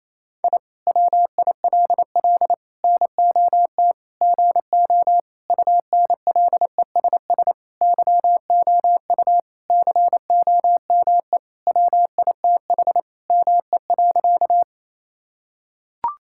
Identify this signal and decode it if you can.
Morse code, heard as a keyed tone, about 28 wpm, 700 Hz. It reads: I WILL NOT GO UNLESS YOU COME WITH ME.
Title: Sentences from Top 1000 Words - Review Part 6 - 28wpm